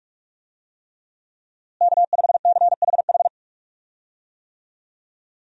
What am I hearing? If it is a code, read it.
Morse code, heard as a keyed tone, about 46 wpm, 700 Hz. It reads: K5CHH
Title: USA Call Signs - 5 Characters - Repeated 3x - 45wpm